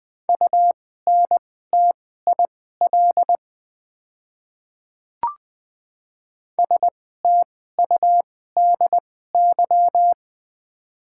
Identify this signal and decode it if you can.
Morse code, heard as a keyed tone, about 20 wpm, spaced as for 14 wpm, 700 Hz. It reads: UNTIL STUDY
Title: 2x Character Spacing - Top 500 Words - 20wpm